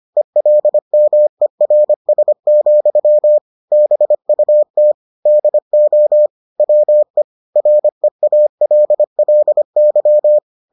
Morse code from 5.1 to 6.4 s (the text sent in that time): DO